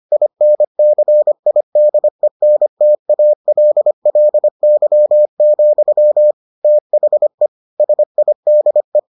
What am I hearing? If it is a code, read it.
Morse code, heard as a keyed tone, about 25 wpm, 600 Hz. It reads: INCIDENTALLY, THE SIDE